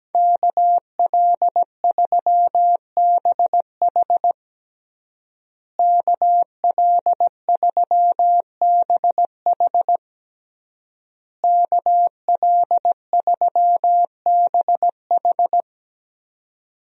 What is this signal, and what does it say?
Morse code, heard as a keyed tone, about 17 wpm, 700 Hz. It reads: KL3BH KL3BH KL3BH
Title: USA Call Signs - 5 Characters - Repeated 3x - 17wpm